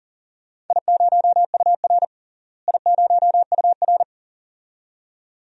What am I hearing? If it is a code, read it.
Morse code, heard as a keyed tone, about 40 wpm, 700 Hz. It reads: I0UR I0UR
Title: DXCC-100 Call Signs - 4 Characters - Repeated 2x - 40wpm